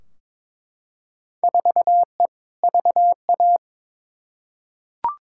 Morse code, heard as a keyed tone, about 22 wpm, 700 Hz. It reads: 4E VA
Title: Field Day Exchanges Encore - 22wpm